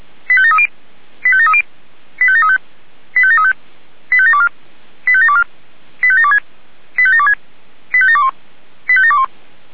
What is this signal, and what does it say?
Five-tone selective call (ZVEI), heard as quick runs of five tones, tones of about 70 ms. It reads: 85730 85730 85735 85735 85724 85724 85726 85726 85721 85721